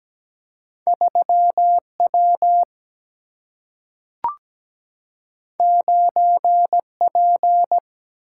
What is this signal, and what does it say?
Morse code, heard as a keyed tone, about 17 wpm, 700 Hz. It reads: 3W 9P
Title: Single-Letter Single-Number 17wpm